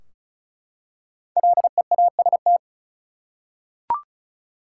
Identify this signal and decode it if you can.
Morse code, heard as a keyed tone, about 36 wpm, 700 Hz. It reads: LEAST
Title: Five Letter Words - 35wpm